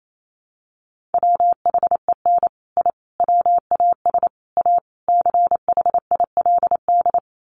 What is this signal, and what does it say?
Morse code, heard as a keyed tone, about 28 wpm, 700 Hz. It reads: WHEN I WAS A CHILD
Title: Sets of 5 Words Encore 28wpm